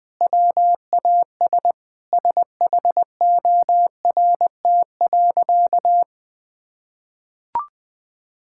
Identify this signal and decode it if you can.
Morse code, heard as a keyed tone, about 20 wpm, 700 Hz. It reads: WAS SHORT.
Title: Sentences from Top 1600 Words 20wpm